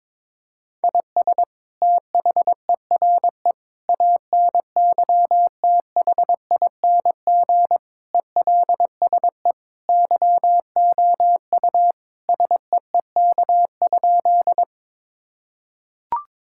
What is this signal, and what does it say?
Morse code, heard as a keyed tone, about 22 wpm, 700 Hz. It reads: IS THERE ANYTHING ELSE YOU SEEK?
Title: Sentences from Top 1600 Words 22wpm